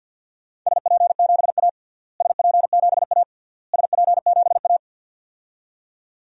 Morse code, heard as a keyed tone, about 46 wpm, 700 Hz. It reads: SP7A SP7A SP7A
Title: DXCC-100 Call Signs - 4 Characters - Repeated 3x - 50wpm